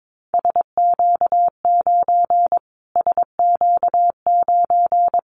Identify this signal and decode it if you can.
Morse code, heard as a keyed tone, about 22 wpm, 700 Hz. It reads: SQ9 SQ9
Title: Callsign Prefixes for Top 25 DXCC - Repeated 3x - 22wpm